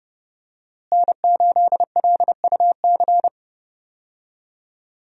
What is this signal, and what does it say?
Morse code, heard as a keyed tone, about 30 wpm, 700 Hz. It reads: N8LUC
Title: USA Call Signs - 5 Characters - Repeated 2x - 30wpm